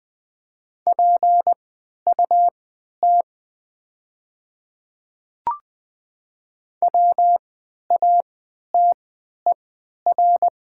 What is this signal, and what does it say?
Morse code, heard as a keyed tone, about 20 wpm, spaced as for 11 wpm, 700 Hz. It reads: PUT WATER
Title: 3x Character Spacing - Top 200 Words - 20wpm